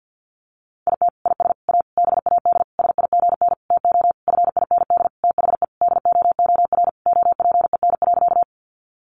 Morse code, heard as a keyed tone, about 50 wpm, 700 Hz. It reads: IT IS A BAD SIGN TO FIND THE DOOR OPEN.
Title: Sentences from Top 1000 Words 50wpm